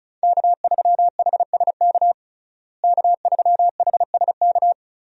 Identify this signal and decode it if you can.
Morse code, heard as a keyed tone, about 36 wpm, 700 Hz. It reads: K3HSK K3HSK
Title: USA Call Signs - 5 Characters - Repeated 2x - 35wpm